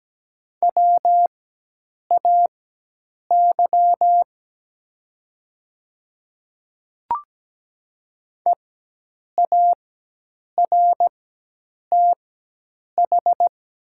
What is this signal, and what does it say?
Morse code, heard as a keyed tone, about 17 wpm, spaced as for 8 wpm, 700 Hz. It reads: WAY EARTH
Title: 4x Character Spacing - Top 200 Words - 17wpm